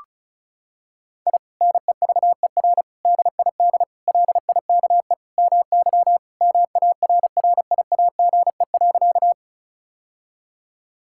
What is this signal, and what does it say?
Morse code, heard as a keyed tone, about 36 wpm, 700 Hz. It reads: I NEVER DID LIKE MY MARRIAGE.
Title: Sentences from Top 1000 Words - Review Part 5 - 35wpm